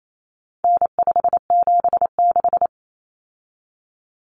Morse code, heard as a keyed tone, about 28 wpm, 700 Hz. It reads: N576
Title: Common Cut Numbers - 4-Digit 28wpm